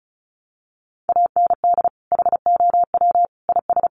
Morse code, heard as a keyed tone, about 34 wpm, 700 Hz. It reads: AND HOW IS
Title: Sentences from Top 1400 Words 35wpm